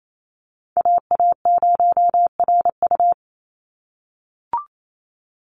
Morse code, heard as a keyed tone, about 28 wpm, 700 Hz. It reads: AA0RU
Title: USA Call Signs 28wpm